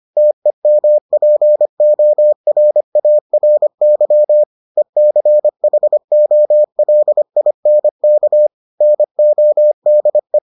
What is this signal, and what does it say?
Morse code, heard as a keyed tone, about 25 wpm, 600 Hz. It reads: TEMPORARY ECHOLINK NODE